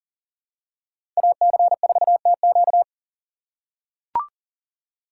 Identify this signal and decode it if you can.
Morse code, heard as a keyed tone, about 40 wpm, 700 Hz. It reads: AC4TQ
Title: USA Call Signs - 5 Characters - 3x Character Spacing - 40wpm